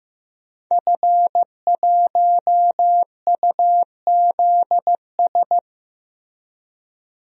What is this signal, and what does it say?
Morse code, heard as a keyed tone, about 15 wpm, 700 Hz. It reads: F1UZS
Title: DXCC-100 Call Signs - 5 Characters - Repeated 3x - 15wpm